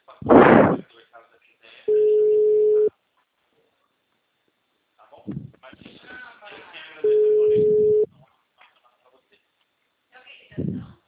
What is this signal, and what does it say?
Ringback tone (Europe, 425 Hz), with speech.
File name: out- -2016